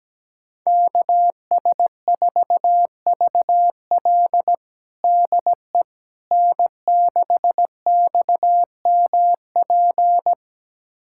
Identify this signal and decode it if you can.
Morse code, heard as a keyed tone, about 17 wpm, 700 Hz. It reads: KS4VL DE N6XMP
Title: USA Callsign DE Callsign - 5 Characters - 17wpm